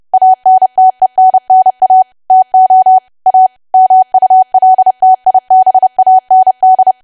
Morse code, heard as a keyed tone, about 30 wpm, 750 Hz. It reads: ANTENNA TO A MULTIBAND
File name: cwa / W1AW-Level3 / 30WPM